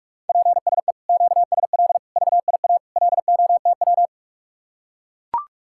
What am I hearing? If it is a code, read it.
Morse code, heard as a keyed tone, about 44 wpm, 700 Hz. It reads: PSE QSL VIA LOTW